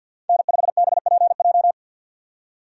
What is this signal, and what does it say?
Morse code, heard as a keyed tone, about 50 wpm, 700 Hz. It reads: N5BPJ
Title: USA Call Signs - 5 Characters - 50wpm